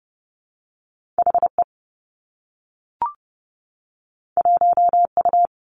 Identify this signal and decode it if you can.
Morse code, heard as a keyed tone, about 30 wpm, 700 Hz. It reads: HE 1U
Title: Two Challenging Letter-Number Combinations 30wpm